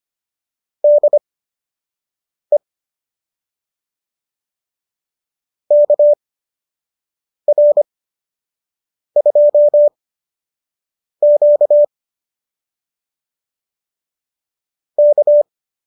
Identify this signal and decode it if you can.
Morse code, heard as a keyed tone, about 25 wpm, spaced as for 6 wpm, 590 Hz. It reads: DE KR2Q K